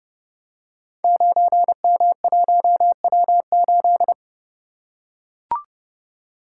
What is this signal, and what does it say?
Morse code, heard as a keyed tone, about 29 wpm, 700 Hz. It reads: 9M1W8
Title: Code Groups - 5 Letter:Numbers - 30wpm